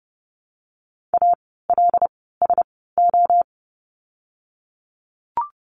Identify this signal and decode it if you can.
Morse code, heard as a keyed tone, about 30 wpm, spaced as for 17 wpm, 700 Hz. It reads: ALSO
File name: course / 30:10wpm / 115 Lesson 19f - Letter w - Words 30wpm